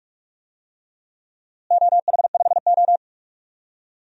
Morse code, heard as a keyed tone, about 43 wpm, 700 Hz. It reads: OH5O
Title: DXCC-100 Call Signs - 4 Characters - Repeated 3x - 45wpm